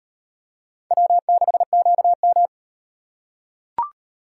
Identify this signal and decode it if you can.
Morse code, heard as a keyed tone, about 36 wpm, 700 Hz. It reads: W6QM